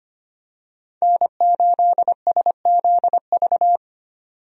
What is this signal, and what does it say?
Morse code, heard as a keyed tone, about 25 wpm, 700 Hz. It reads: N8SZV
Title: USA Call Signs - 5 Characters - 25wpm